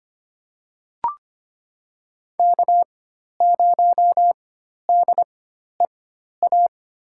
Morse code, heard as a keyed tone, about 25 wpm, spaced as for 12 wpm, 700 Hz. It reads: K0DEA
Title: USA Call Signs - 5 Characters - 4x Character Spacing - 25wpm